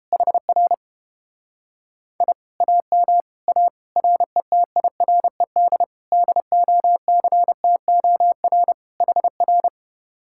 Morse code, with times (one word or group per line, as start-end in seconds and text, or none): 0.1-0.7 HR
2.2-2.3 I
2.6-3.2 AM
3.5-3.7 A
4.0-5.8 RETIRED
6.1-8.7 DOCTOR
9.0-9.7 HR